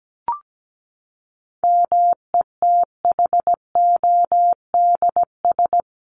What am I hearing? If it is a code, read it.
Morse code, heard as a keyed tone, about 17 wpm, 700 Hz. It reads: METHODS
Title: ES:S Suffix - Top 100 Words with Base Words 17wpm